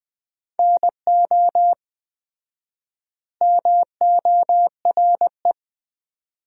Morse code, heard as a keyed tone, about 20 wpm, 700 Hz. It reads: NO MORE
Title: 4x Word-Spacing - Sentences from Top 400 Words - 20wpm